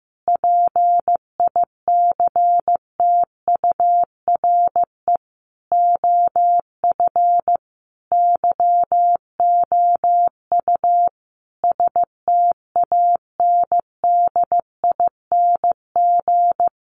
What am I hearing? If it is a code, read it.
Morse code, heard as a keyed tone, about 15 wpm, 700 Hz. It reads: PICTURE OF YOU STANDING
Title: A Sound Of Thunder 15wpm